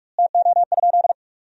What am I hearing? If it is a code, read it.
Morse code, heard as a keyed tone, about 46 wpm, 700 Hz. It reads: TO?
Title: 2x Word-Spacing - Sentences from Top 100 Words - 45wpm